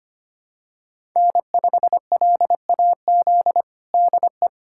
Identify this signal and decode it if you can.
Morse code, heard as a keyed tone, about 25 wpm, 700 Hz. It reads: N5LAZ DE
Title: USA Callsign DE Callsign 25wpm